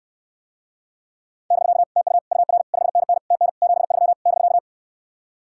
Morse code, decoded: <BK> TU UR 5NN TN 73 <BK>

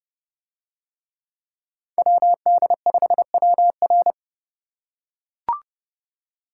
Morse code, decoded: WD5WR

